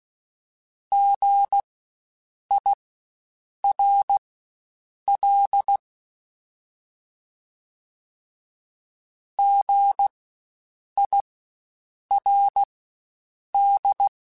Morse code, heard as a keyed tone, about 16 wpm, spaced as for 7 wpm, 800 Hz. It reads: GIRL GIRD
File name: Spaced Alphas / Spaced Alphas mp3s / Gs sp @15